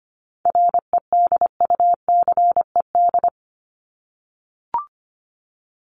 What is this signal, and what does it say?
Morse code, heard as a keyed tone, about 25 wpm, 700 Hz. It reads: REDUCED
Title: ED Suffix - Top 100 Words with Base Words 25wpm